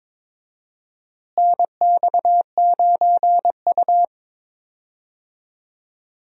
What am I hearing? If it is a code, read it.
Morse code, heard as a keyed tone, about 22 wpm, 700 Hz. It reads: NX9U
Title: USA Call Signs - 4 Characters - Repeated 3x - 22wpm